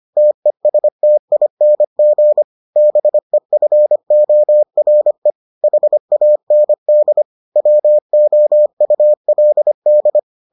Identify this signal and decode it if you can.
Morse code, heard as a keyed tone, about 25 wpm, 600 Hz. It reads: TESTING BEFORE HAND WOULD